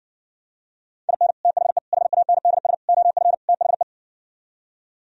Morse code, heard as a keyed tone, about 60 wpm, 700 Hz. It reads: IN THE HANDS OF THE